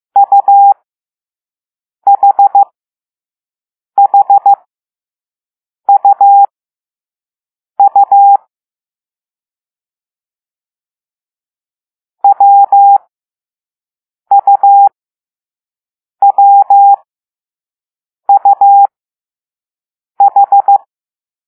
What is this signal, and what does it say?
Morse code, heard as a keyed tone, about 15 wpm, spaced as for 5 wpm, 800 Hz. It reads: UHHUU WUWUH